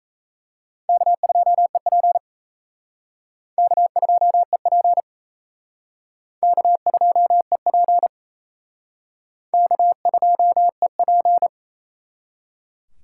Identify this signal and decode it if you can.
Morse code, heard as a keyed tone, about 34 wpm, 700 Hz. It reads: K2EP K2EP K2EP K2EP
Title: Speed Racing - USA 1x2 Call Signs 28wpm